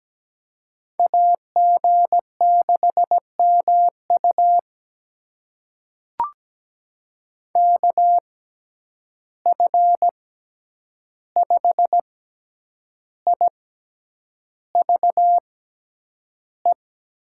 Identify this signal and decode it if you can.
Morse code, heard as a keyed tone, about 17 wpm, spaced as for 8 wpm, 700 Hz. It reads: AG6MU KF5IVE
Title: USA Call Signs - 6x Character Spacing - 17wpm